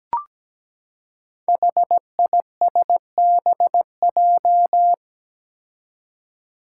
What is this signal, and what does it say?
Morse code, heard as a keyed tone, about 17 wpm, 700 Hz. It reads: HISBJ
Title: Five Challenging Letter-Number Combinations 17wpm